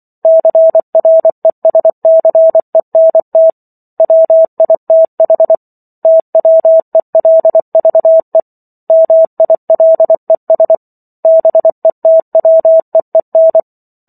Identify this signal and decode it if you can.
Morse code, heard as a keyed tone, about 24 wpm, 650 Hz. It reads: CRESCENT WITH TWELVE MILES BETWEEN